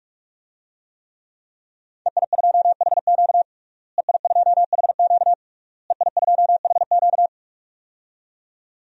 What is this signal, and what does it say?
Morse code, heard as a keyed tone, about 41 wpm, 700 Hz. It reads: EI2HQ EI2HQ EI2HQ
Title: DXCC-100 Call Signs - 5 Characters - Repeated 3x - 45wpm